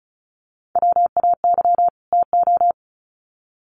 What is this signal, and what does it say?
Morse code, heard as a keyed tone, about 34 wpm, 700 Hz. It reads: WAY TO
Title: Sets of 2 Words 35wpm